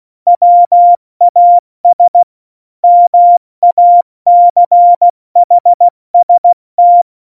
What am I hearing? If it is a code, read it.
Morse code, heard as a keyed tone, about 16 wpm, 700 Hz. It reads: WAS MACHST
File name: Morsekurs / Witze / Witz 14. Januar 2020, 70 F80